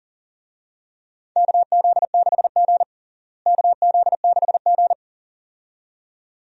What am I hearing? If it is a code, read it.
Morse code, heard as a keyed tone, about 37 wpm, 700 Hz. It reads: KZ6G KZ6G
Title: USA Call Signs - 4 Characters - Repeated 2x - 40wpm